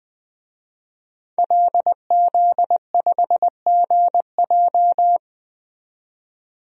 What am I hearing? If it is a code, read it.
Morse code, heard as a keyed tone, about 20 wpm, 700 Hz. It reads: LZ5GJ
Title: DXCC-100 Call Signs - 5 Characters - 20wpm